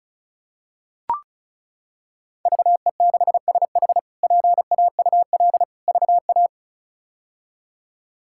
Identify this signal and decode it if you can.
Morse code, heard as a keyed tone, about 36 wpm, 700 Hz. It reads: VE6SH PAUL VA
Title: K1USN SST QSO Practice - 35wpm